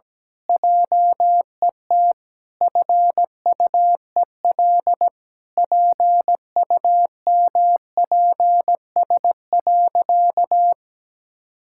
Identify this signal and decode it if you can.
Morse code, heard as a keyed tone, about 17 wpm, 700 Hz. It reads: JET FUEL PUMPS.
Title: Pangram Challenge v2 - 17wpm